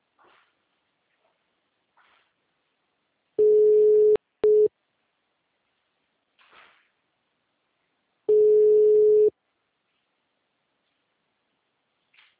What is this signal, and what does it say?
Ringback tone (Europe, 425 Hz).
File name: out- -2006- - - .48